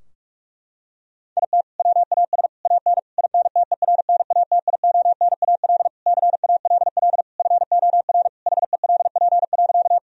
Morse code, heard as a keyed tone, about 46 wpm, 700 Hz. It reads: IT WAS AN INTERNATIONAL CALL FOR HELP.
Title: Sentences from Top 1000 Words - Review Part 4 - 45wpm